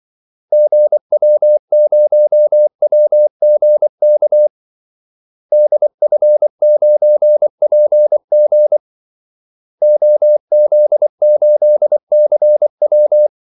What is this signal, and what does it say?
Morse code, heard as a keyed tone, about 24 wpm, 600 Hz. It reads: GW0WGK DF9PG OZ8CW
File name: Callsigns-24wpm